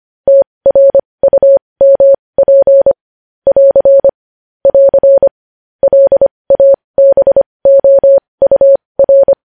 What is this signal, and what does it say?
Morse code, heard as a keyed tone, about 25 wpm, spaced as for 20 wpm, 550 Hz. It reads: TRUMP <AR> <AR> LABOUR
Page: news-headlines-in-morse-code-at-20-wpm